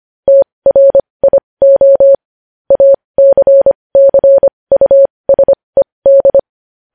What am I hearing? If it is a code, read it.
Morse code, heard as a keyed tone, about 25 wpm, spaced as for 20 wpm, 550 Hz. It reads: TRIO ACCUSED
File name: podcast / News headlines-1x-25x20